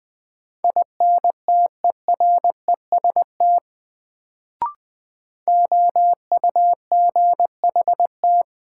Morse code, heard as a keyed tone, about 20 wpm, 700 Hz. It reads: INTEREST OUGHT
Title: Sets of 5 Words Encore - New Word Drill - 20wpm